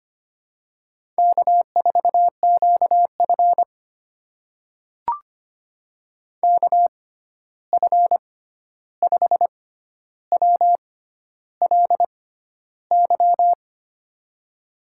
Morse code, heard as a keyed tone, about 25 wpm, spaced as for 11 wpm, 700 Hz. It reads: K4QF KF5WLY
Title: USA Call Signs - 6x Character Spacing - 25wpm